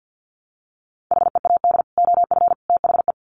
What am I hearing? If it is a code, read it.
Morse code, heard as a keyed tone, about 50 wpm, 700 Hz. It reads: HEAD OF THE
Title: Sets of 3 Words 50wpm